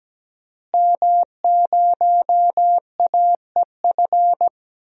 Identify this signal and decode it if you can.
Morse code, heard as a keyed tone, about 17 wpm, 700 Hz. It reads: M0AEF
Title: DXCC-100 Call Signs - 5 Characters - Repeated 2x - 17wpm